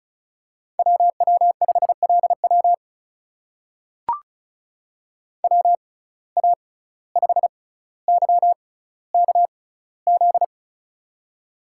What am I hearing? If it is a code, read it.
Morse code, heard as a keyed tone, about 36 wpm, spaced as for 17 wpm, 700 Hz. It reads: WW5LW WA5YKZ
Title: USA Call Signs - 6x Character Spacing - 35wpm